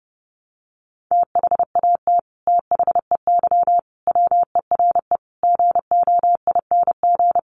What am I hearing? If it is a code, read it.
Morse code, heard as a keyed tone, about 30 wpm, 700 Hz. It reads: THAT THEY WERE GOING
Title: Sets of 5 Words Encore 30wpm